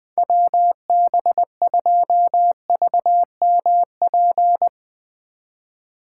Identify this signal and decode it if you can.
Morse code, heard as a keyed tone, about 22 wpm, 700 Hz. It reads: WB2VMP
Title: Speed Racing - USA Call Signs 20wpm